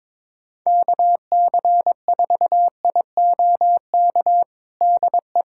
Morse code, heard as a keyed tone, about 22 wpm, 700 Hz. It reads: KC4IOK DE
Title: USA Callsign DE Callsign - 6 Characters - 22wpm